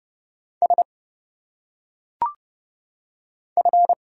Morse code, heard as a keyed tone, about 29 wpm, 700 Hz. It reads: S F
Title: Single Letters - Phonetic Pronunciation 30wpm